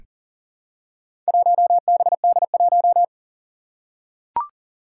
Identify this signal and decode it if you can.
Morse code, heard as a keyed tone, about 39 wpm, 700 Hz. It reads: 1BD1
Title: Four Challenging Letter-Number Combinations 40wpm